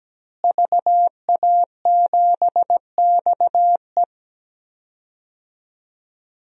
Morse code, heard as a keyed tone, about 17 wpm, 700 Hz. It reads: VA7XE
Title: DXCC-100 Call Signs - 5 Characters - Repeated 3x - 17wpm